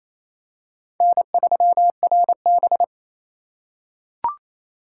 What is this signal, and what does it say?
Morse code, heard as a keyed tone, about 27 wpm, 700 Hz. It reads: N3RB
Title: USA Call Signs - 4 Characters - 2x Character Spacing - 28wpm